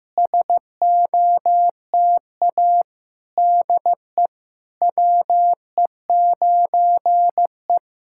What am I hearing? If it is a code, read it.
Morse code, heard as a keyed tone, about 16 wpm, 700 Hz. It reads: SOTA DE WE9E
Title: SOTA QSO Practice 15wpm